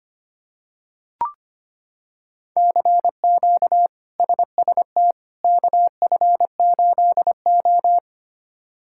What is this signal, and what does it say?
Morse code, heard as a keyed tone, about 25 wpm, 700 Hz. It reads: CQ SST KF8O
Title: K1USN SST QSO Practice - 25wpm